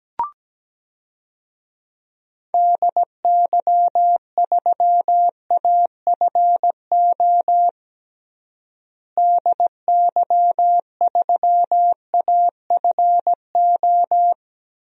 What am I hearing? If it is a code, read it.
Morse code, heard as a keyed tone, about 17 wpm, 700 Hz. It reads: DY3AFO DY3AFO
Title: DXCC-100 Call Signs - 6 Characters - Repeated 2x - 17wpm